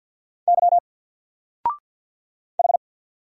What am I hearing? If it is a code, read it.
Morse code, heard as a keyed tone, about 49 wpm, 700 Hz. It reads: Y H
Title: Single Character (Letter:Number:Punctuation:Prosign) 50wpm